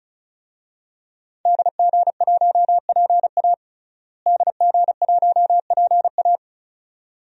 Morse code, heard as a keyed tone, about 36 wpm, 700 Hz. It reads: DG1PA DG1PA
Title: DXCC-100 Call Signs - 5 Characters - Repeated 2x - 35wpm